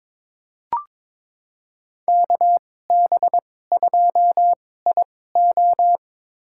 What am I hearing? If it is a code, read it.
Morse code, heard as a keyed tone, about 21 wpm, spaced as for 16 wpm, 700 Hz. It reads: KB2IO
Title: USA Call Signs - 5 Characters - 2x Character Spacing - 22wpm